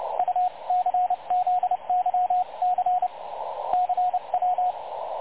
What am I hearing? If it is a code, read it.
Morse code, heard as a keyed tone, about 29 wpm, spaced as for 25 wpm, 730 Hz, with radio noise. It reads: UCZYC CW
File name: dlazaawansowanych / nagraniazpasma / websdr recording 3565.0kHz